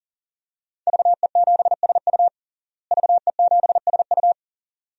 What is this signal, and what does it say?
Morse code, heard as a keyed tone, about 40 wpm, 700 Hz. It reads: VE7SU VE7SU
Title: DXCC-100 Call Signs - 5 Characters - Repeated 2x - 40wpm